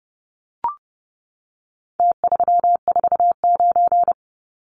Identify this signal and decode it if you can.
Morse code, heard as a keyed tone, about 30 wpm, 700 Hz. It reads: T349
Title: Numbers - Contest Style - 4-Digit 30wpm